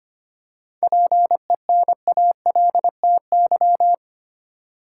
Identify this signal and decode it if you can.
Morse code, heard as a keyed tone, about 25 wpm, 700 Hz. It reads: PENALTY
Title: Seven Letter Words - 25wpm